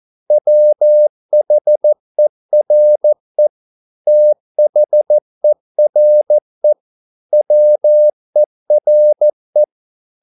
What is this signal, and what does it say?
Morse code, heard as a keyed tone, about 14 wpm, 600 Hz. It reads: WHERE THERE WERE